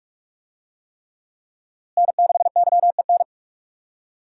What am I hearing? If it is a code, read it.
Morse code, heard as a keyed tone, about 43 wpm, 700 Hz. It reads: N6YEN
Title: USA Call Signs - 5 Characters - Repeated 3x - 45wpm